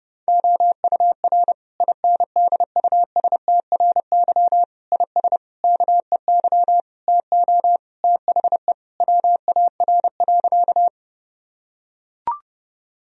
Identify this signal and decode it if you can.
Morse code, heard as a keyed tone, about 30 wpm, 700 Hz. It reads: OUR INDUSTRY IS KEY TO THE WAR.
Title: Sentences from Top 1000 Words - Review Part 4 - 30wpm